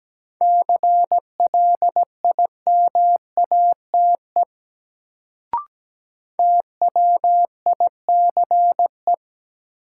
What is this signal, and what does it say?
Morse code, heard as a keyed tone, about 17 wpm, 700 Hz. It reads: CLIMATE TWICE
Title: Sentences from Top 1600 Words - New Word Drill - 17wpm